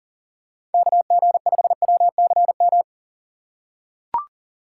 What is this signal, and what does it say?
Morse code, heard as a keyed tone, about 40 wpm, 700 Hz. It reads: KG5WCM
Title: USA Call Signs - 3x Character Spacing - 40wpm